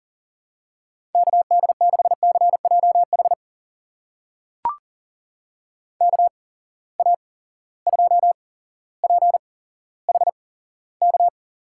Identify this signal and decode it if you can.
Morse code, heard as a keyed tone, about 39 wpm, spaced as for 17 wpm, 700 Hz. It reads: KD6CJH KA2PHK
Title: USA Call Signs - 6 Characters - 8x Character Spacing - 40wpm